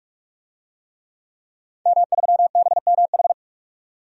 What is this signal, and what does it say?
Morse code, heard as a keyed tone, about 46 wpm, 700 Hz. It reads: M3BMH
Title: DXCC-100 Call Signs - 5 Characters - 45wpm